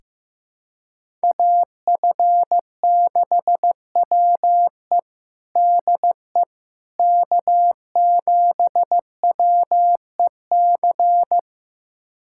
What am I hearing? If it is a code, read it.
Morse code, heard as a keyed tone, about 16 wpm, 700 Hz. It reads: AF6WE DE K7WEC
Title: USA Callsign DE Callsign - 5 Characters - 15wpm